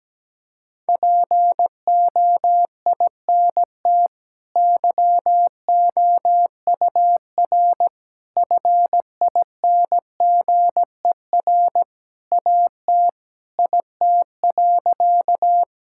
Morse code, heard as a keyed tone, about 17 wpm, 700 Hz. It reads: POINT YOUR FINGER AT IT.
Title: Sentences from Top 1600 Words 17wpm